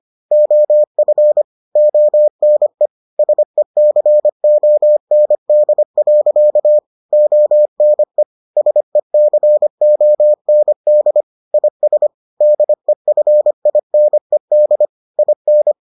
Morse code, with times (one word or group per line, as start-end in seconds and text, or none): 0.3-1.4 OF
1.8-2.9 ONE
3.2-6.8 SECOND.
7.1-8.2 ONE
8.6-11.2 SECOND
11.5-12.1 IS
12.4-14.9 DEFINED
15.2-15.7 IN